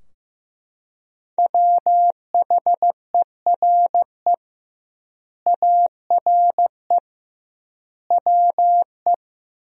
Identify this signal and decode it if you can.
Morse code, heard as a keyed tone, about 15 wpm, 700 Hz. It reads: WHERE ARE WE
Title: 2x Word-Spacing - Sentences from Top 200 Words - 15wpm